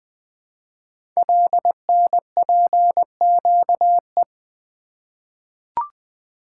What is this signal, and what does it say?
Morse code, heard as a keyed tone, about 20 wpm, 700 Hz. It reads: LNPQE
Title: Code Groups - 5 Letters - 20wpm